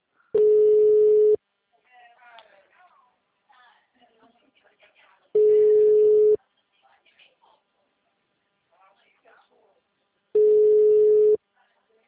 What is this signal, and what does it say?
Ringback tone (Europe, 425 Hz), with speech.